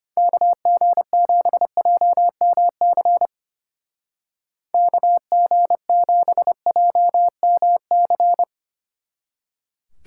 Morse code, heard as a keyed tone, about 29 wpm, 700 Hz. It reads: KG7JMC KG7JMC
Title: Speed Racing - USA 2x3 Call Signs 25wpm